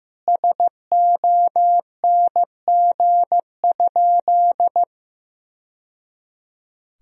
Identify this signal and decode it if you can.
Morse code, heard as a keyed tone, about 15 wpm, 700 Hz. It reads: SONG?
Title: Sentences from Top 1000 Words - Review Part 2 - 15wpm